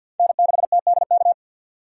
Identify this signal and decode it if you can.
Morse code, heard as a keyed tone, about 50 wpm, 700 Hz. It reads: N6TDK
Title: USA Call Signs - 5 Characters - 4x Character Spacing - 50wpm